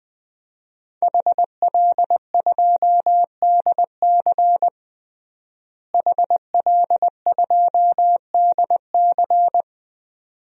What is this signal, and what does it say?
Morse code, heard as a keyed tone, about 21 wpm, 700 Hz. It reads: HL2DC HL2DC
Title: DXCC-100 Call Signs - 5 Characters - Repeated 2x - 20wpm